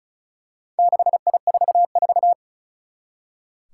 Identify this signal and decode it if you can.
Morse code, heard as a keyed tone, about 36 wpm, 700 Hz. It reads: 6I44
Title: Four Challenging Letter-Number Combinations 35wpm